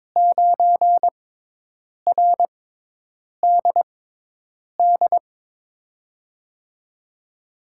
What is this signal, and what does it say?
Morse code, heard as a keyed tone, about 22 wpm, spaced as for 8 wpm, 700 Hz. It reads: 9RDD
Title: USA Call Signs - 5 Characters - 6x Character Spacing - 22wpm